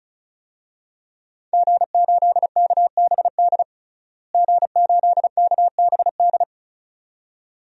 Morse code, heard as a keyed tone, about 34 wpm, 700 Hz. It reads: G8KBD G8KBD
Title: DXCC-100 Call Signs - 5 Characters - Repeated 2x - 35wpm